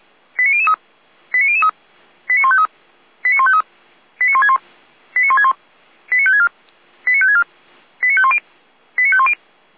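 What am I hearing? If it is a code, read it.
Five-tone selective call (ZVEI), heard as quick runs of five tones, tones of about 70 ms. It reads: 89003 89003 89153 89153 89161 89161 89564 89564 89510 89510
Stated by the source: ZVEI five-tone selective call received by SDR, from audio.f85360.feuerwehr-orlamuende.org fire brigade server.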